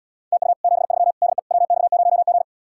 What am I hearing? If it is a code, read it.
Morse code, heard as a keyed tone, about 80 wpm, 700 Hz. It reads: TU 73 DE WX0G